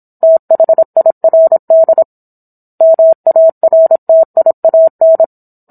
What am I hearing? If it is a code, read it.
Morse code, heard as a keyed tone, about 27 wpm, 650 Hz. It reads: THIRD MARTIAN